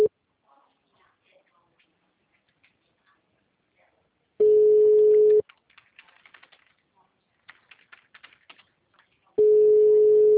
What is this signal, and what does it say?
Ringback tone (Europe, 425 Hz), with speech.